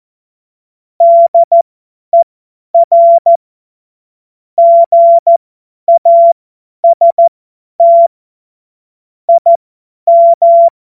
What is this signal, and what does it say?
Morse code, heard as a keyed tone, about 14 wpm, spaced as for 10 wpm, 680 Hz. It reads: DER GAST IM